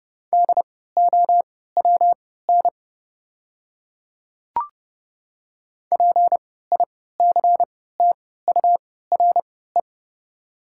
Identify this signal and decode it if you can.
Morse code, heard as a keyed tone, about 29 wpm, spaced as for 17 wpm, 690 Hz. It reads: DOWN PICTURE
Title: 3x Character Spacing - Top 200 Words - 30wpm